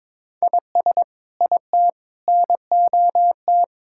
Morse code, heard as a keyed tone, about 21 wpm, 700 Hz. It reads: IS IT NOT
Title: Sentences from Top 1600 Words 22wpm